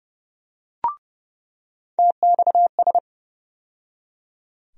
Morse code, heard as a keyed tone, about 30 wpm, 700 Hz. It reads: TXS
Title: USA Call Sign Postfixes - 3 Letters 30wpm